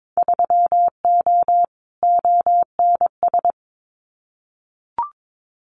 Morse code, heard as a keyed tone, about 22 wpm, 700 Hz. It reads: 3O ONS
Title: Winter Field Day Exchanges 22wpm